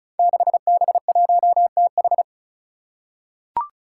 Morse code, heard as a keyed tone, about 36 wpm, 700 Hz. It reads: 6B1TH
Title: Five Challenging Letter-Number Combinations 35wpm